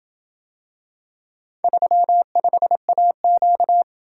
Morse code, heard as a keyed tone, about 26 wpm, 700 Hz. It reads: 35AQ